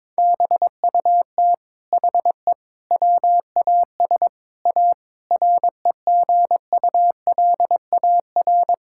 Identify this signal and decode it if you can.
Morse code, heard as a keyed tone, about 22 wpm, 700 Hz. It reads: BUT HE WAS A REGULAR
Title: Sentences from Top 1000 Words - Review Part 6 - 22wpm